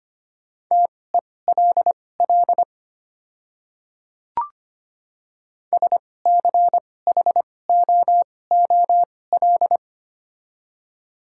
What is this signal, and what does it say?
Morse code, heard as a keyed tone, about 25 wpm, spaced as for 18 wpm, 690 Hz. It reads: TELL SCHOOL
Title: 2x Character Spacing - Top 300 Words - 25wpm